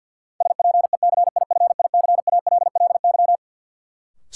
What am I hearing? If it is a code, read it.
Morse code, heard as a keyed tone, about 48 wpm, 700 Hz. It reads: SPECIFICALLY
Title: Sentences from Top 1100 Words - New Word Drill - 50wpm